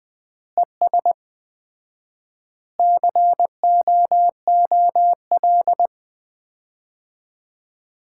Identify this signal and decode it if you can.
Morse code, heard as a keyed tone, about 20 wpm, 700 Hz. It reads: ES COOL
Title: Common QSO Phrases - 4x Word-Spacing - 20wpm